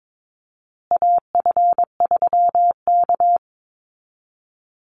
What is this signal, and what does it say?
Morse code, heard as a keyed tone, about 22 wpm, 700 Hz. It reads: AF3K
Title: Call Signs Repeated 2x - 22wpm